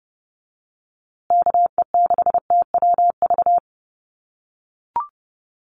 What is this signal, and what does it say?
Morse code, heard as a keyed tone, about 29 wpm, 700 Hz. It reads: KE6TWV
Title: USA Call Signs Repeated 2x 30wpm